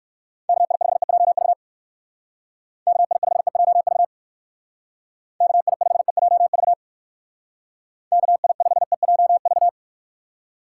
Morse code, heard as a keyed tone, about 57 wpm, 700 Hz. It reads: KI5EJV KI5EJV KI5EJV KI5EJV